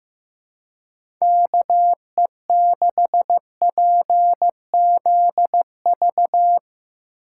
Morse code, heard as a keyed tone, about 15 wpm, 700 Hz. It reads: KE6PZV
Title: USA Call Signs - 6 Characters - 2x Character Spacing - 15wpm